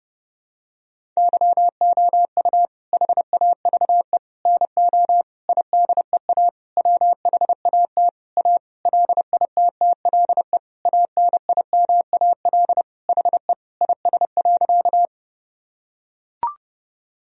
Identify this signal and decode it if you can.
Morse code, heard as a keyed tone, about 30 wpm, 700 Hz. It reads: YOU HAVE NO IDEA WHAT A LITTLE ANIMAL HE IS.
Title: Sentences from Top 1000 Words - Review Part 1 - 30wpm